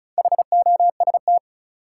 Morse code, heard as a keyed tone, about 36 wpm, 700 Hz. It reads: HOST
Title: Four Letter Words - 35wpm